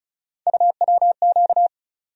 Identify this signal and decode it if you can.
Morse code, heard as a keyed tone, about 36 wpm, 700 Hz. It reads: UWQ
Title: USA Call Sign Postfixes - 3 Letters 35wpm